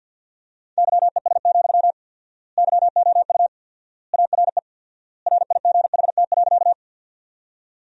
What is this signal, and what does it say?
Morse code, heard as a keyed tone, about 53 wpm, 700 Hz. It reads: YES, YOU ARE RIGHT.
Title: 4x Word-Spacing - Sentences from Top 500 Words - 50wpm